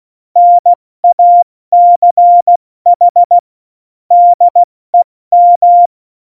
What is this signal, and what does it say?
Morse code, heard as a keyed tone, about 16 wpm, 700 Hz. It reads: NACH DEM
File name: Morsekurs / Witze / Witz 29. Januar 2019, 70 F80